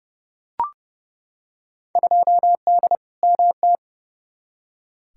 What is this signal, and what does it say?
Morse code, heard as a keyed tone, about 30 wpm, 700 Hz. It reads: 2D MT
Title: Field Day Exchanges Encore - 30wpm